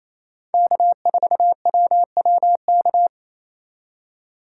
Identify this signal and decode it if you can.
Morse code, heard as a keyed tone, about 28 wpm, 700 Hz. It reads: K4WWK
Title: USA Call Signs - 5 Characters - 2x Character Spacing - 28wpm